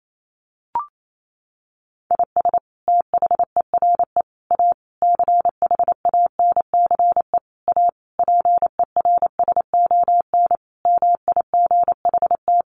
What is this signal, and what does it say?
Morse code, heard as a keyed tone, about 28 wpm, 700 Hz. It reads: IS THERE A CHANCE A PERSON MIGHT
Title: Sentences from Top 900 Words 28wpm